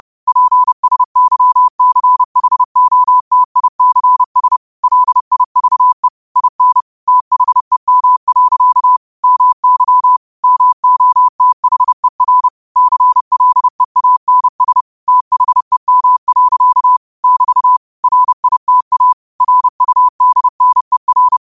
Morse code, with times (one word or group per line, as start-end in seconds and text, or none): 0.3-4.6 PSYCHOTICS
4.8-6.1 LIVE
6.4-6.8 IN
7.1-9.0 THEM.
9.2-10.2 MY
10.4-12.5 MOTHER
12.8-14.8 CLEANS
15.1-17.0 THEM.
17.2-17.8 <BT>
18.0-19.1 RITA
19.4-21.4 RUDNER